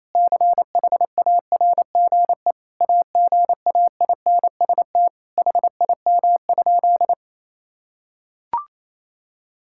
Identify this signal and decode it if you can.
Morse code, heard as a keyed tone, about 28 wpm, 700 Hz. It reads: CHARGE AGAINST HIM?
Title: Sentences from Top 1000 Words - Review Part 6 - 28wpm